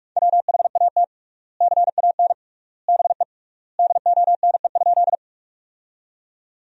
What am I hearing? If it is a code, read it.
Morse code, heard as a keyed tone, about 46 wpm, 700 Hz. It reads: WHAT CAN BE DONE?
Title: 3x Word-Spacing - Sentences from Top 500 Words - 45wpm